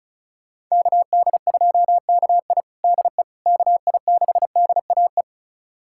Morse code, heard as a keyed tone, about 34 wpm, 700 Hz. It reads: KD2KI DE KI6DAE
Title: USA Callsign DE Callsign 35wpm